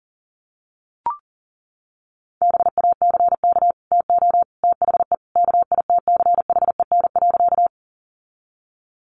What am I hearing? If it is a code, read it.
Morse code, heard as a keyed tone, about 37 wpm, 700 Hz. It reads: BACK TO THE KITCHEN.
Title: Sentences from Top 1400 Words 40wpm